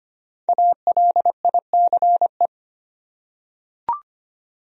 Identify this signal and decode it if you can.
Morse code, heard as a keyed tone, about 25 wpm, 700 Hz. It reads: ALICE
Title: USA Common Names - 25wpm